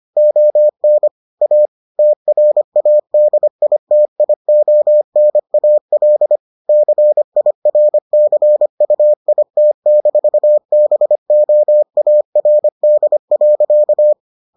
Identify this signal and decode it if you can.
Morse code, heard as a keyed tone, about 25 wpm, 600 Hz. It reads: ON A TRADITIONAL CIRCUIT-BOARD.